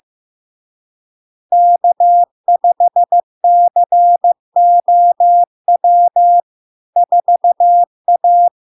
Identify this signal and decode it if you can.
Morse code, heard as a keyed tone, about 15 wpm, 700 Hz. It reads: K5COW 4A
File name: LARK-FD-2022-Log-in-CW-15wpm-700-Hz